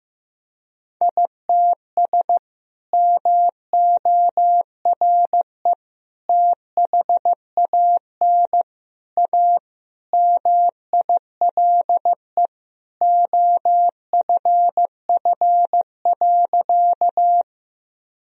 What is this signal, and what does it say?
Morse code, heard as a keyed tone, about 15 wpm, 700 Hz. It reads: ITS MORE THAN A MILE OFF.
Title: Sentences from Top 1000 Words - Review Part 1 - 15wpm